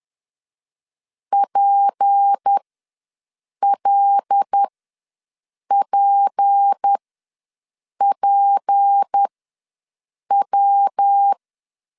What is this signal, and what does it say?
Morse code, heard as a keyed tone, about 10 wpm, spaced as for 6 wpm, 800 Hz. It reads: PLPPW
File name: LEZIONI / 6 PKX0 / 6 P ws 10wpm 3751